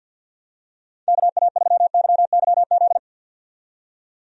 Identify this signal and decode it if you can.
Morse code, heard as a keyed tone, about 53 wpm, 700 Hz. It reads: KA3YYZ
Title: USA Call Signs - 6 Characters - 50wpm